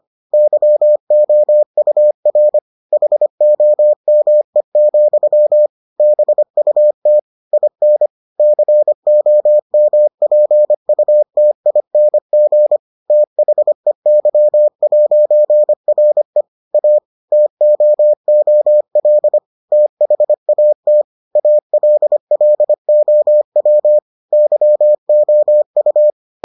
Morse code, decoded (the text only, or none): YOUR HOME, BUT IN COMPUTING THEY'RE A TOOL THAT ALLOW YOU